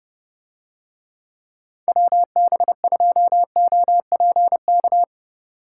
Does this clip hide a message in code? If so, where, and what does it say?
Yes: Morse code from 1.9 to 5.0 s, heard as a keyed tone, about 30 wpm, 700 Hz. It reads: WB2OPK